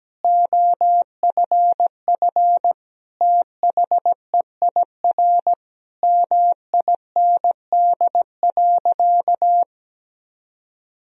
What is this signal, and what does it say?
Morse code, heard as a keyed tone, about 17 wpm, 700 Hz. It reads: OFF THEIR MIND.
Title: Sentences from Top 1000 Words - Review Part 3 - 17wpm